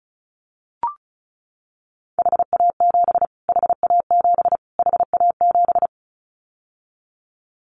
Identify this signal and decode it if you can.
Morse code, heard as a keyed tone, about 34 wpm, 700 Hz. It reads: HA7 HA7 HA7
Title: Callsign Prefixes for Top 25 DXCC - Repeated 3x - 35wpm